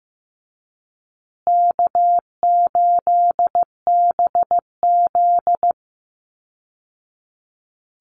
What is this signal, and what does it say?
Morse code, heard as a keyed tone, about 15 wpm, 700 Hz. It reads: K8BZ
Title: Easy Call Signs Repeated 2x - 15wpm